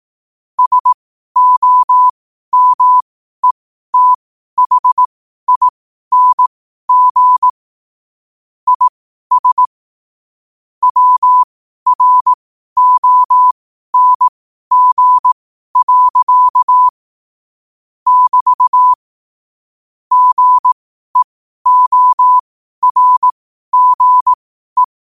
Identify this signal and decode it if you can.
Morse code, heard as a keyed tone, about 18 wpm, spaced as for 12 wpm, 1000 Hz. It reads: SOMETHING IS WRONG. <BT> GEORGE
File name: qotdfiles / 12wpm